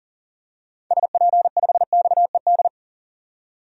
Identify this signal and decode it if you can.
Morse code, heard as a keyed tone, about 40 wpm, 700 Hz. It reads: SP5XED
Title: DXCC-100 Call Signs - 6 Characters - 40wpm